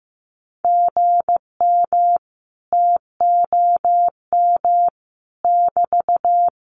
Morse code, heard as a keyed tone, about 15 wpm, 700 Hz. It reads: GM TOM <BT>
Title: Common QSO Phrases Encore 15wpm